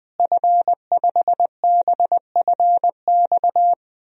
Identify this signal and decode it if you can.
Morse code, heard as a keyed tone, about 19 wpm, 700 Hz. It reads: F5BFX
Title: DXCC-100 Call Signs - 5 Characters - Repeated 3x - 20wpm